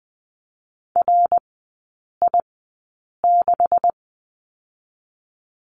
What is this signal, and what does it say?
Morse code, heard as a keyed tone, about 20 wpm, spaced as for 8 wpm, 700 Hz. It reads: RI6
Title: 127 Lesson 21e - Number 6 - Three Characters 20wpm